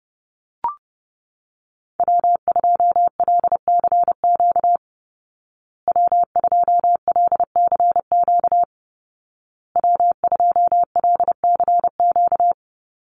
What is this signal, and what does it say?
Morse code, heard as a keyed tone, about 30 wpm, 700 Hz. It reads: W2LCQ W2LCQ W2LCQ
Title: Call Signs Repeated 2x - 30wpm